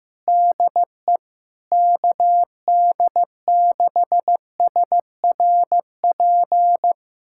Morse code, heard as a keyed tone, about 15 wpm, 700 Hz. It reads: DE KD6SRP
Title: USA Callsign DE Callsign 15wpm